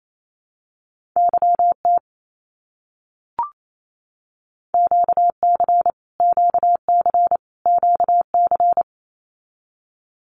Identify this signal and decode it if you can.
Morse code, heard as a keyed tone, about 28 wpm, 700 Hz. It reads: YT QC QC QC
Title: Canadian Provinces and Territories - Repeated 3x - 28wpm